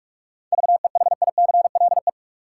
Morse code, heard as a keyed tone, about 46 wpm, 700 Hz. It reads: VEHICLE
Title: Seven Letter Words - 45wpm